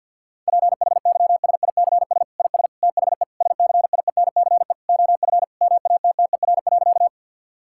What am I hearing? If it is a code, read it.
Morse code, heard as a keyed tone, about 50 wpm, 700 Hz. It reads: PHYSICS IS THE SCIENCE OF MATTER.
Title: Sentences from Top 1000 Words - Review Part 5 - 50wpm